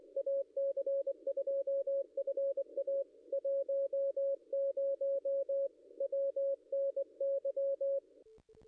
Morse code, heard as a keyed tone, about 23 wpm, 550 Hz. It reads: AC2FA 10 WNY
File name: contests / ss-cw-k5tr-op-n5rz -0100